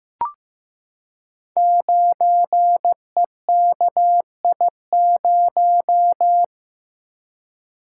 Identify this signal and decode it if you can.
Morse code, heard as a keyed tone, about 15 wpm, 700 Hz. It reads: 9EKI0
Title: Code Groups - 5 Letter:Numbers - 15wpm